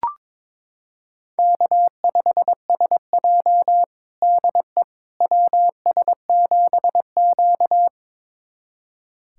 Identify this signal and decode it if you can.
Morse code, heard as a keyed tone, about 22 wpm, 700 Hz. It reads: K5SJ DE WS7Q
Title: USA Callsign DE Callsign - 4 Characters - 22wpm